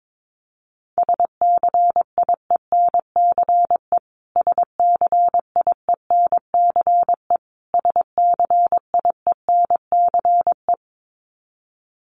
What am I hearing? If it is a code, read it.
Morse code, heard as a keyed tone, about 22 wpm, 700 Hz. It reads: SCIENCE SCIENCE SCIENCE
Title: Top 500 Words Repeated 22wpm